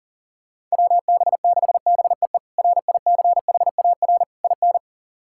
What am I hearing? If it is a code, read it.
Morse code, heard as a keyed tone, about 40 wpm, 690 Hz. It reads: WB6BEE RICHAR IN
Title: K1USN SST QSO Practice - 40wpm